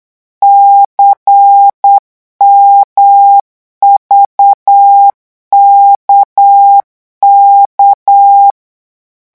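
Morse code, heard as a keyed tone, about 8 wpm, 800 Hz. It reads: CMVKK
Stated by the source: Morse code revision lesson at 8 words per minute, from a French amateur radio course.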